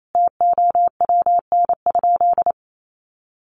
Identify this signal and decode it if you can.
Morse code, heard as a keyed tone, about 28 wpm, 700 Hz. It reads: TOWN?
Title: Sentences from Top 1300 Words 28wpm